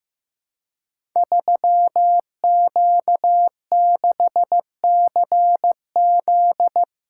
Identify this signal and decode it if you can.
Morse code, heard as a keyed tone, about 15 wpm, 700 Hz. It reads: 3Q6CZ
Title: Code Groups - 5 Letter:Numbers - 15wpm